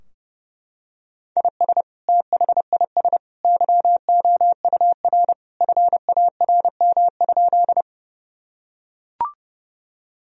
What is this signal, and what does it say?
Morse code, heard as a keyed tone, about 30 wpm, 700 Hz. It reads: IS THIS YOUR FARM?
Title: Sentences from Top 1000 Words - Review Part 2 - 30wpm